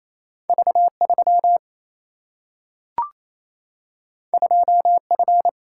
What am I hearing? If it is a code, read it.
Morse code, heard as a keyed tone, about 28 wpm, 700 Hz. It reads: V 3 2 F
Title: Single-Letter Single-Number 28wpm